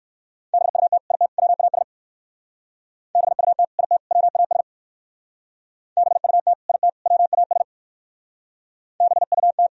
Morse code, heard as a keyed tone, about 61 wpm, 700 Hz. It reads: BUT IT WAS BUT IT WAS BUT IT WAS BUT